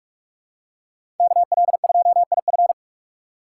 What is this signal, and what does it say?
Morse code, heard as a keyed tone, about 47 wpm, 700 Hz. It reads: KL2IF
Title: USA Call Signs - 5 Characters - 2x Character Spacing - 45wpm